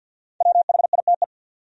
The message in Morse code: WHITE

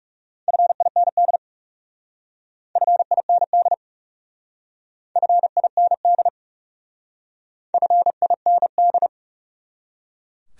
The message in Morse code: FIND FIND FIND FIND

